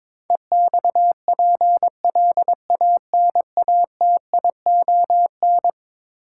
Morse code, heard as a keyed tone, about 22 wpm, 700 Hz. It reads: EXPLANATION
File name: practice / 22wpm / Eleven Letter Words - 22wpm